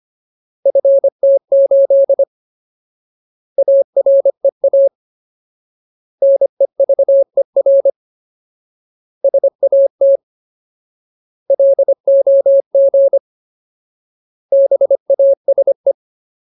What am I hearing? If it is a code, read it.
Morse code, heard as a keyed tone, about 25 wpm, 550 Hz. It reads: FT8 AREA NEVER SAT LOG BASE
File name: Ultimate Word List / 25WPM Ultimate